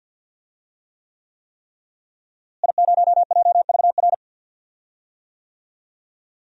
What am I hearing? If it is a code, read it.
Morse code, heard as a keyed tone, about 47 wpm, 700 Hz. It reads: I0JVR
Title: DXCC-100 Call Signs - 5 Characters - Repeated 3x - 50wpm